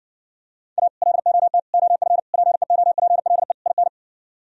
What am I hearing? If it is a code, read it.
Morse code, heard as a keyed tone, about 69 wpm, 700 Hz. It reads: A LOT OF PEOPLE IN